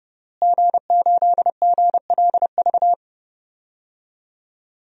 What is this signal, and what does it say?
Morse code, heard as a keyed tone, about 30 wpm, 700 Hz. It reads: G8GLV
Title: DXCC-100 Call Signs - 5 Characters - Repeated 3x - 30wpm